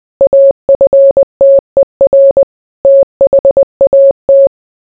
Morse code, heard as a keyed tone, about 20 wpm, 550 Hz. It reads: AFTER THAT